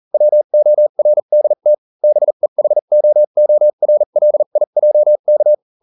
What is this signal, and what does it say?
Morse code, heard as a keyed tone, about 40 wpm, spaced as for 35 wpm, 600 Hz. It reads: WORDT BEHOORLIJK